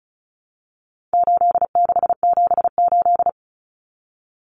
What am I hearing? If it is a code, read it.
Morse code, heard as a keyed tone, about 34 wpm, 700 Hz. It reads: 8678
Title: Numbers - 4-Digit 35wpm